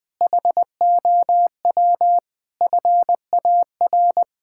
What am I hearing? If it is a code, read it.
Morse code, heard as a keyed tone, about 20 wpm, 700 Hz. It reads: HOW FAR
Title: Sentences from Top 1600 Words 20wpm